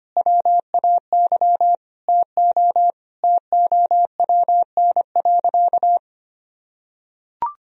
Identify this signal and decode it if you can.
Morse code, heard as a keyed tone, about 25 wpm, 700 Hz. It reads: WAY TO TOWN.
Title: Sentences from Top 1000 Words - Review Part 1 - 25wpm